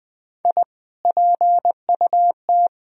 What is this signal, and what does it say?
Morse code, heard as a keyed tone, about 20 wpm, 700 Hz. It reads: I PUT
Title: Sentences from Top 1000 Words - Review Part 1 - 20wpm